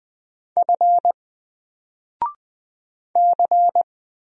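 Morse code, heard as a keyed tone, about 20 wpm, 700 Hz. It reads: F C
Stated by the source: Single letter and number Morse drill at 20 wpm, from N4LCD.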